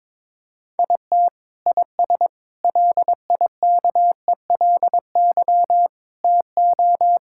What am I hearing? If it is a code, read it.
Morse code, heard as a keyed tone, about 21 wpm, 700 Hz. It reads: IT IS LIKELY TO
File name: practice / 22wpm / Sentences with 2-Word Phrases - 22wpm